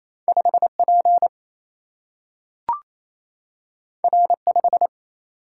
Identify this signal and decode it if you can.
Morse code, heard as a keyed tone, about 28 wpm, 700 Hz. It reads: 5P R5
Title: Single-Letter Single-Number 28wpm